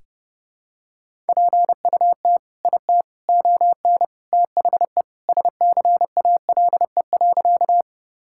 Morse code, heard as a keyed tone, about 32 wpm, 700 Hz. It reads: PUT IT ON THE SCALE.
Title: Sentences from Top 1000 Words - Review Part 5 - 30wpm